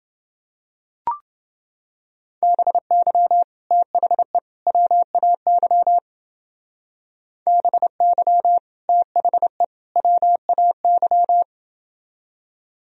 Morse code, heard as a keyed tone, about 28 wpm, 700 Hz. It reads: BY THE WAY BY THE WAY